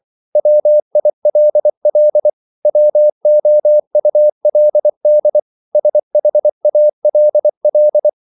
Morse code, heard as a keyed tone, about 24 wpm, 600 Hz. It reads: WILL WOULD SHALL
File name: CW / Practice / 100MostCommonEnglishWords24